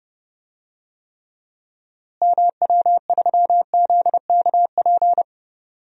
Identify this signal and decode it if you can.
Morse code, heard as a keyed tone, about 30 wpm, 700 Hz. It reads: MW3ZKP
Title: DXCC-100 Call Signs - 6 Characters - Repeated 2x - 30wpm